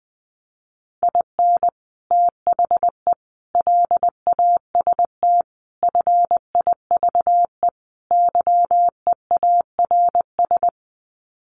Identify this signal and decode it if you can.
Morse code, heard as a keyed tone, about 20 wpm, 700 Hz. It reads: IN THE LAST FIVE YEARS
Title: Sets of 5 Words Encore 20wpm